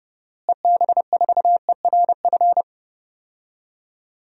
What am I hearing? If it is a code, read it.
Morse code, heard as a keyed tone, about 30 wpm, 700 Hz. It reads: EB4ERF
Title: DXCC-100 Call Signs - 6 Characters - Repeated 2x - 30wpm